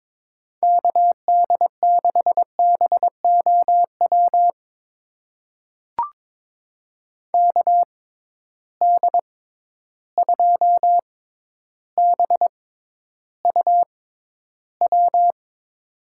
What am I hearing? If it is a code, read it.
Morse code, heard as a keyed tone, about 22 wpm, spaced as for 11 wpm, 700 Hz. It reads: KD6BOW KD2BUW
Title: USA Call Signs - 6 Characters - 6x Character Spacing - 22wpm